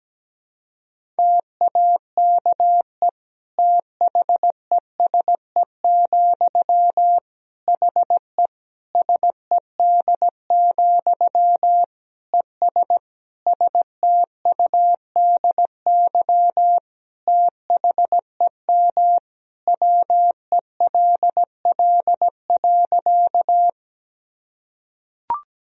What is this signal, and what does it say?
Morse code, heard as a keyed tone, about 17 wpm, 700 Hz. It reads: TAKE THESE, HE SED, ES STUDY THEM WELL.
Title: Sentences from Top 300 Words and Abbreviations 17wpm